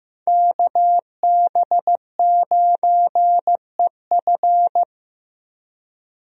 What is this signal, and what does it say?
Morse code, heard as a keyed tone, about 15 wpm, 700 Hz. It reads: KB9EF
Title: USA Call Signs - 5 Characters - 15wpm